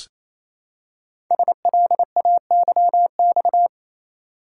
Morse code, heard as a keyed tone, about 27 wpm, 700 Hz. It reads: SLAYX
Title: Code Groups - 5 Letters - 28wpm